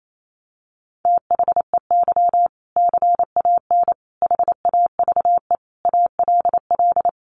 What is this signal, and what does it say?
Morse code, heard as a keyed tone, about 28 wpm, 700 Hz. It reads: THEY CAN HAVE ALL
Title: Sentences from Top 400 Words 28wpm